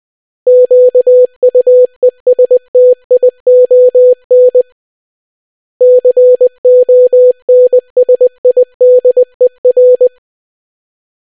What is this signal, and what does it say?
Morse code, heard as a keyed tone, about 20 wpm, 500 Hz. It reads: QUESTION CONSIDER